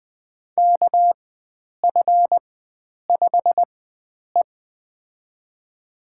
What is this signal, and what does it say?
Morse code, heard as a keyed tone, about 21 wpm, spaced as for 9 wpm, 700 Hz. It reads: KF5E
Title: USA Call Signs - 4 Characters - 4x Character Spacing - 20wpm